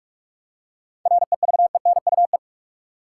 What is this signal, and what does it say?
Morse code, heard as a keyed tone, about 46 wpm, 700 Hz. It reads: REVENUE
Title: Seven Letter Words - 45wpm